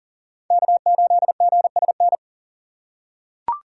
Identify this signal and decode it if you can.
Morse code, heard as a keyed tone, about 40 wpm, 700 Hz. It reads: K8GSN
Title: USA Call Signs - 5 Characters - 6x Character Spacing - 40wpm